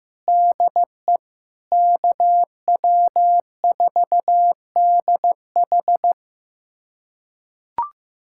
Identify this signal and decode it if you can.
Morse code, heard as a keyed tone, about 15 wpm, 700 Hz. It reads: DE KW4DH
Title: USA Callsign DE Callsign - 5 Characters - 15wpm